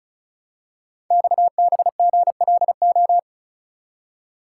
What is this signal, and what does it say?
Morse code, heard as a keyed tone, about 36 wpm, 700 Hz. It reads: XBGLO
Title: Code Groups - 5 Letter:Numbers - 35wpm